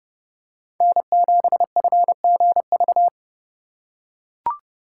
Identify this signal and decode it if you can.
Morse code, heard as a keyed tone, about 29 wpm, 700 Hz. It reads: N7FGV